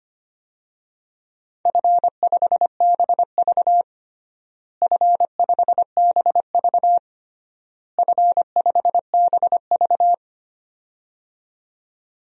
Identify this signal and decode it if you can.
Morse code, heard as a keyed tone, about 25 wpm, 700 Hz. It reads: F5BV F5BV F5BV
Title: DXCC-100 Call Signs - 4 Characters - Repeated 3x - 25wpm